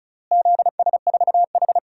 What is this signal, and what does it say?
Morse code, heard as a keyed tone, about 34 wpm, 700 Hz. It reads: ZS4H
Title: DXCC-100 Call Signs - 4 Characters - Repeated 3x - 35wpm